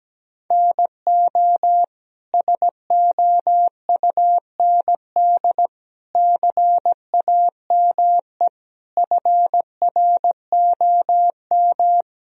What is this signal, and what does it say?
Morse code, heard as a keyed tone, about 17 wpm, 700 Hz. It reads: NO SOUND CAME FROM
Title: Sentences from Top 1000 Words - Review Part 4 - 17wpm